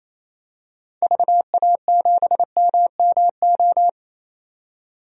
Morse code, heard as a keyed tone, about 28 wpm, 700 Hz. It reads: VA7MMO
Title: DXCC-100 Call Signs - 6 Characters - 28wpm